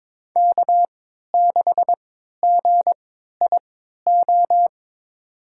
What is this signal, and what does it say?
Morse code, heard as a keyed tone, about 21 wpm, spaced as for 12 wpm, 700 Hz. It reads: K6GIO
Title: USA Call Signs - 5 Characters - 3x Character Spacing - 22wpm